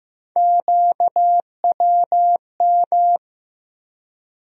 Morse code, heard as a keyed tone, about 15 wpm, 700 Hz. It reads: QWM
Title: USA Call Sign Postfixes - 3 Letters 15wpm